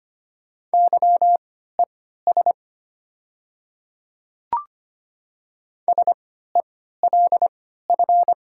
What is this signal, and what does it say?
Morse code, heard as a keyed tone, about 25 wpm, spaced as for 14 wpm, 700 Hz. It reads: YES SELF
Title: 3x Character Spacing - Top 500 Words - 25wpm